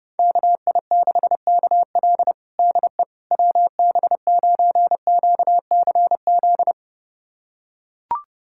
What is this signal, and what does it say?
Morse code, heard as a keyed tone, about 30 wpm, 700 Hz. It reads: KI6KL DE WB9QCZ
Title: USA Callsign DE Callsign 30wpm